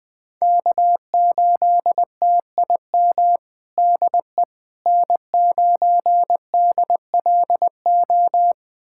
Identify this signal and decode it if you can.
Morse code, heard as a keyed tone, about 19 wpm, 700 Hz. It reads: K8TIM DE N9DLO
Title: USA Callsign DE Callsign - 5 Characters - 20wpm